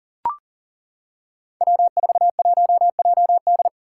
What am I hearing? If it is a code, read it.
Morse code, heard as a keyed tone, about 40 wpm, 700 Hz. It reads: W41JD
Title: Five Challenging Letter-Number Combinations 40wpm